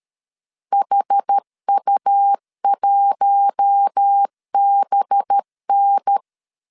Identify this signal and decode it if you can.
Morse code, heard as a keyed tone, about 13 wpm, 800 Hz. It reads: HU1BN
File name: CW / DET / 1 6 1500C S A 12wpm 3440